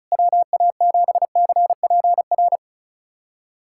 Morse code, heard as a keyed tone, about 34 wpm, 700 Hz. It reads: WA7CPR
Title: USA Callsign DE Callsign 35wpm